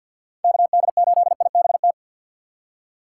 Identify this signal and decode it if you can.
Morse code, heard as a keyed tone, about 50 wpm, 700 Hz. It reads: KD8IBT
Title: USA Call Signs - 6 Characters - 6x Character Spacing - 50wpm